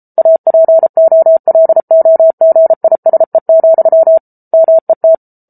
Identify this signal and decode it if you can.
Morse code, heard as a keyed tone, about 34 wpm, 650 Hz. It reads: APOLOGISE, MET